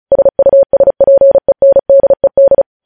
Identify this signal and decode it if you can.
Morse code, heard as a keyed tone, about 36 wpm, 550 Hz. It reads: SUSPENDED
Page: news headlines in morse code at 35 wpm